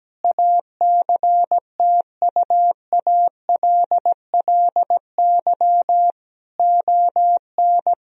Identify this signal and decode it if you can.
Morse code, heard as a keyed tone, about 17 wpm, 700 Hz. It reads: ACTUALLY ON
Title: Sentences from Top 1000 Words - Review Part 4 - 17wpm